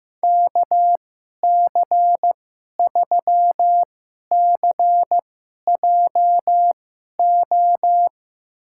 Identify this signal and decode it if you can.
Morse code, heard as a keyed tone, about 15 wpm, spaced as for 11 wpm, 700 Hz. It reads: KC3CJO
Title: USA Call Signs - 2x Character Spacing - 15wpm